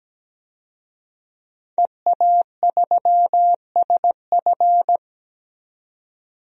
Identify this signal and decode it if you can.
Morse code, heard as a keyed tone, about 17 wpm, 700 Hz. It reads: EA3SF
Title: DXCC-100 Call Signs - 5 Characters - 17wpm